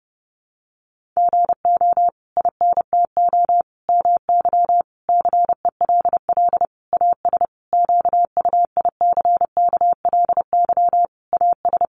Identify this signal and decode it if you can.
Morse code, heard as a keyed tone, about 30 wpm, 700 Hz. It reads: GO INTO MY CELL AS QUICKLY AS
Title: Sentences from Top 1100 Words 30wpm